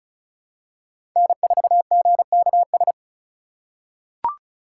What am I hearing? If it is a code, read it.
Morse code, heard as a keyed tone, about 36 wpm, 700 Hz. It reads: N4GKS
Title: USA Call Signs - 5 Characters - 2x Character Spacing - 35wpm